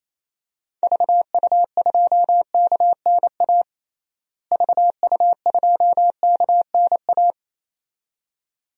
Morse code, heard as a keyed tone, about 28 wpm, 700 Hz. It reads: VU2KNA VU2KNA
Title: DXCC-100 Call Signs - 6 Characters - Repeated 2x - 28wpm